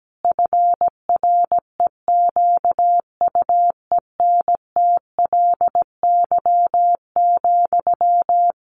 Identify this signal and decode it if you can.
Morse code, heard as a keyed tone, about 17 wpm, 700 Hz. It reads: FREQUENTLY,
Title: Sentences from Top 1300 Words 17wpm